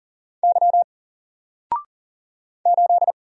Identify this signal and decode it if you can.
Morse code, heard as a keyed tone, about 40 wpm, 700 Hz. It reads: Y 8
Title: Single Character (Letter:Number:Punctuation:Prosign) 40wpm